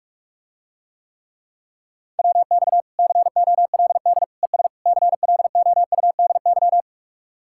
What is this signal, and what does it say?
Morse code, heard as a keyed tone, about 47 wpm, 700 Hz. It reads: WX COLD ES CLOUDY